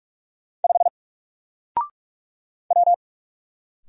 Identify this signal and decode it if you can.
Morse code, heard as a keyed tone, about 45 wpm, 700 Hz. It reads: 5 W
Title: Single Letter-Number 45wpm